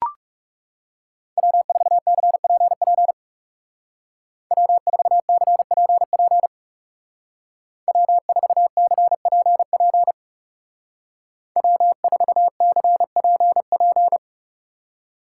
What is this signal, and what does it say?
Morse code, heard as a keyed tone, about 35 wpm, 700 Hz. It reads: W4CPP W4CPP W4CPP W4CPP
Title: Speed Racing - USA 1x3 Call Signs 30wpm